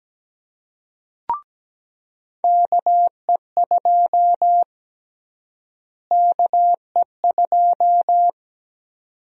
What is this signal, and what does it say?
Morse code, heard as a keyed tone, about 17 wpm, 700 Hz. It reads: KE2 KE2
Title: USA Call Sign Prefixes - Repeated 2x - 17wpm